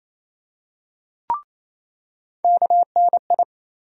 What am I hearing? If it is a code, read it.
Morse code, heard as a keyed tone, about 28 wpm, 700 Hz. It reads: KNI
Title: USA Call Sign Postfixes - 3 Letters 28wpm